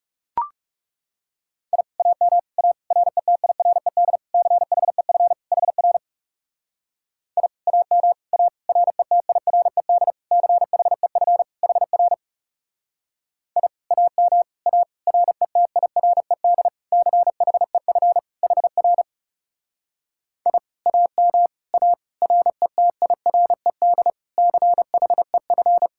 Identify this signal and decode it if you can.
Morse code, heard as a keyed tone, about 38 wpm, 700 Hz. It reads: I AM A RETIRED CHEF HR I AM A RETIRED CHEF HR I AM A RETIRED CHEF HR I AM A RETIRED CHEF